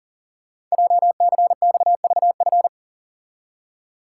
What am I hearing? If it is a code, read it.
Morse code, heard as a keyed tone, about 42 wpm, 690 Hz. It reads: JCXVF
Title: Code Groups - 5 Letters - 40wpm